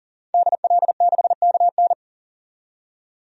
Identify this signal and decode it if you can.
Morse code, heard as a keyed tone, about 40 wpm, 700 Hz. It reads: DL6KN
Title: DXCC-100 Call Signs - 5 Characters - 40wpm